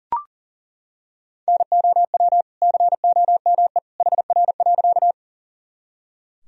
Morse code, heard as a keyed tone, about 37 wpm, 700 Hz. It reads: NOW COME HR.
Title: Sentences from Top 200 Words and Abbreviations 40wpm